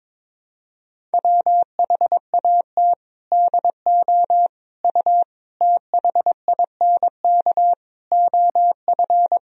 Morse code, heard as a keyed tone, about 22 wpm, 700 Hz. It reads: WHAT DO U THINK OF
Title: Sentences from Top 200 Words and Abbreviations 22wpm